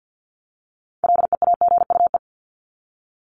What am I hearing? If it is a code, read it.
Morse code, heard as a keyed tone, about 50 wpm, 700 Hz. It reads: LEAGUE
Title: Top 1751-2000 Words 50wpm